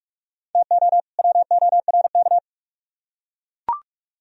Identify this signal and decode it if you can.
Morse code, heard as a keyed tone, about 40 wpm, 700 Hz. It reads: TO WORK